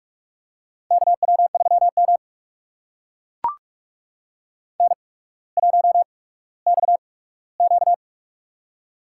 Morse code, heard as a keyed tone, about 44 wpm, spaced as for 19 wpm, 700 Hz. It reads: KW3M N1XQ
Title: USA Call Signs - 4 Characters - 8x Character Spacing - 45wpm